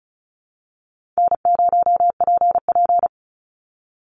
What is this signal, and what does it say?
Morse code, heard as a keyed tone, about 34 wpm, 700 Hz. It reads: N0PP
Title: Call Signs - 35wpm